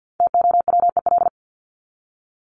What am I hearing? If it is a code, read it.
Morse code, heard as a keyed tone, about 48 wpm, 700 Hz. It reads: TOWEL
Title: Top 1501-1750 Words 50wpm